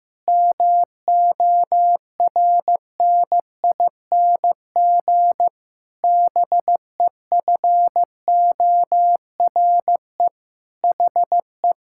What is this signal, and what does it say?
Morse code, heard as a keyed tone, about 15 wpm, 700 Hz. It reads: MORNING BEFORE HE
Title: Pangram Challenge v2 - 15wpm